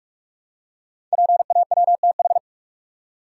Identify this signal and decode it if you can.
Morse code, heard as a keyed tone, about 44 wpm, 700 Hz. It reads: PAWTH
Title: Code Groups - 5 Letters - 45wpm